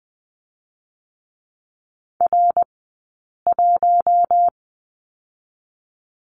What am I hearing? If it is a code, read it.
Morse code, heard as a keyed tone, about 20 wpm, 700 Hz. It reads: R 1
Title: 052 Lesson 10d - Letter r - Two Characters 20wpm